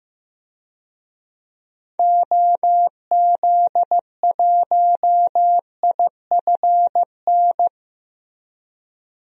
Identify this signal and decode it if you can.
Morse code, heard as a keyed tone, about 15 wpm, 700 Hz. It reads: OZ1IFN
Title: DXCC-100 Call Signs - 6 Characters - Repeated 3x - 15wpm